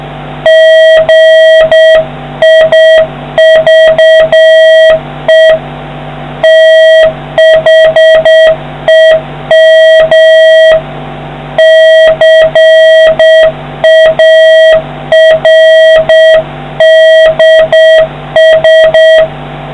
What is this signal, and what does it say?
Morse code, heard as a keyed tone, about 7 wpm, 1900 Hz, with noise. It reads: GIVE THEM CARDS